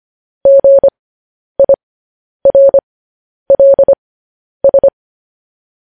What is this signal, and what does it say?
Morse code, heard as a keyed tone, about 25 wpm, spaced as for 10 wpm, 550 Hz. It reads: GIRLS